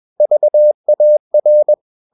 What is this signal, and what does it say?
Morse code, heard as a keyed tone, about 21 wpm, 600 Hz. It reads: VAR
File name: Ferie QRS 21wpm